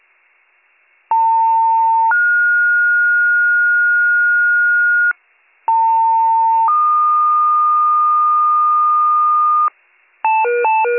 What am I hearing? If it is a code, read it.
Two-tone page, heard as a one-second tone, then a three-second tone, about 900 then 1430 Hz, about 900 then 1220 Hz, with noise. It is heard from 1.1 to 9.7 s.